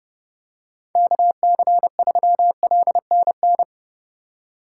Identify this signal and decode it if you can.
Morse code, heard as a keyed tone, about 32 wpm, 700 Hz. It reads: KC3LNN
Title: USA Call Signs - 6 Characters - 3x Character Spacing - 30wpm